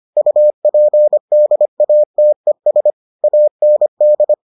Morse code, heard as a keyed tone, about 25 wpm, 600 Hz. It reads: UPDATES AND